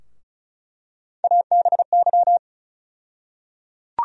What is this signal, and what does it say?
Morse code, heard as a keyed tone, about 36 wpm, 700 Hz. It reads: ABY
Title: USA Call Sign Postfixes - 3 Letters 35wpm